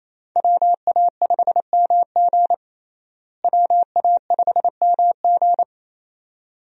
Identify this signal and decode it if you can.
Morse code, heard as a keyed tone, about 28 wpm, 700 Hz. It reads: WA5MG WA5MG
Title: USA Call Signs - 5 Characters - Repeated 2x - 28wpm